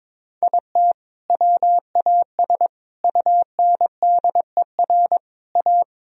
Morse code, heard as a keyed tone, about 21 wpm, 700 Hz. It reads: IT WAS UNDER A
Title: Sentences from Top 1000 Words - Review Part 1 - 22wpm